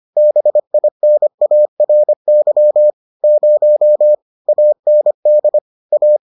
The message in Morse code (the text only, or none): BINARY 0 AND A